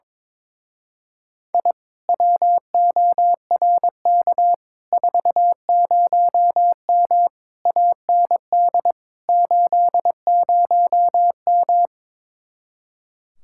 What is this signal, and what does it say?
Morse code, heard as a keyed tone, about 22 wpm, 700 Hz. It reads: I WORK 40M AND 80M
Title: Speed-Racing Common QSO Phrases - 22wpm